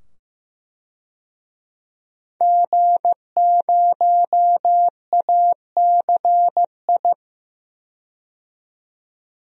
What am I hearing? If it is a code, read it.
Morse code, heard as a keyed tone, about 15 wpm, 700 Hz. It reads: G0ACI
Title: DXCC-100 Call Signs - 5 Characters - 15wpm